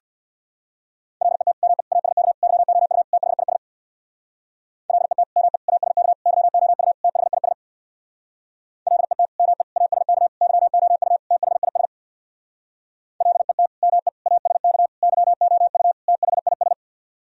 Morse code, heard as a keyed tone, about 64 wpm, 700 Hz. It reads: LET ME ASK YOU THIS LET ME ASK YOU THIS LET ME ASK YOU THIS LET ME ASK YOU THIS